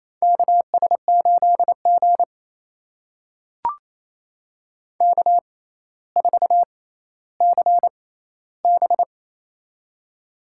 Morse code, heard as a keyed tone, about 28 wpm, spaced as for 14 wpm, 700 Hz. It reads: KS8G K4CB
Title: USA Call Signs - 4 Characters - 6x Character Spacing - 28wpm